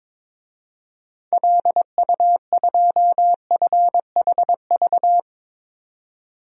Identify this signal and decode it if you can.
Morse code, heard as a keyed tone, about 22 wpm, 700 Hz. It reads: LU2FHV
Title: DXCC-100 Call Signs - 6 Characters - 22wpm